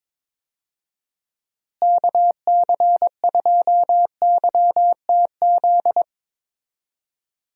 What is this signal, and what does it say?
Morse code, heard as a keyed tone, about 22 wpm, 700 Hz. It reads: KC2YTZ
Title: USA Call Signs - 6 Characters - Repeated 2x - 22wpm